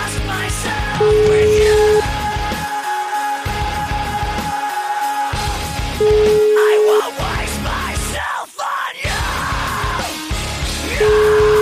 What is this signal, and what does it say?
Ringback tone (Europe, 425 Hz), with noise.